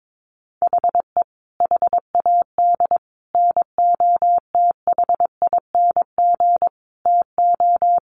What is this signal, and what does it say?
Morse code, heard as a keyed tone, about 22 wpm, 700 Hz. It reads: HE HAD NOTHING TO
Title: Sentences from Top 1100 Words 22wpm